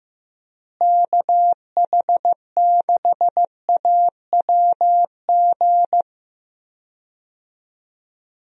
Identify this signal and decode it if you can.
Morse code, heard as a keyed tone, about 15 wpm, 700 Hz. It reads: KH6AWG
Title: DXCC-100 Call Signs - 6 Characters - Repeated 2x - 15wpm